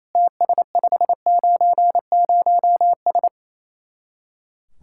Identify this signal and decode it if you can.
Morse code, heard as a keyed tone, about 35 wpm, 700 Hz. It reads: TS590S
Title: Speed-Racing Common QSO Phrases - 28wpm